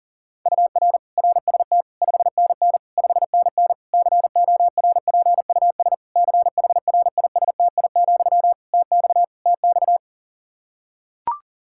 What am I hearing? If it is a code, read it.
Morse code, heard as a keyed tone, about 40 wpm, 700 Hz. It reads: UR RST 5NN 5NN CORPUS CHRISTI, TX TX
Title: K1USN SST QSO Practice 40wpm